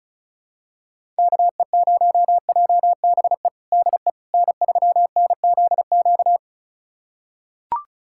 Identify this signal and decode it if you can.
Morse code, heard as a keyed tone, about 36 wpm, 700 Hz. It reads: KE0JBE DE N3NZQ
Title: USA Callsign DE Callsign 35wpm